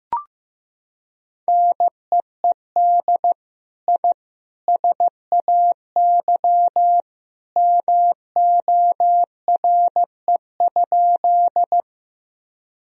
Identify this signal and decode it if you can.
Morse code, heard as a keyed tone, about 15 wpm, 700 Hz. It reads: NEED I SAY MORE?
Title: Sentences from Top 1000 Words - Review Part 1 - 15wpm